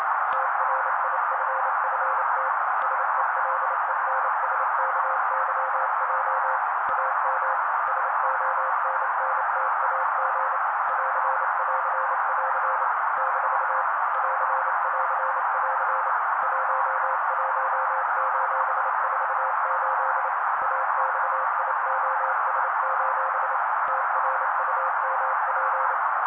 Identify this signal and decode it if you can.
Morse code, heard as a keyed tone, about 28 wpm, 530 Hz. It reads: TRIFFT SELENSKYJ AM SONNTAG <AR><AR><AR> - <AR><AR><AR> 11:48 AXIOS: TRUMP
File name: daily / 27wpm noise